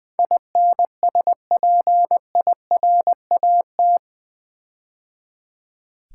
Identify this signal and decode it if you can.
Morse code, heard as a keyed tone, about 20 wpm, 700 Hz. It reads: INSPIRAT
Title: Build A Word - 4 Letters At A Time - 11-Letter Words - 20wpm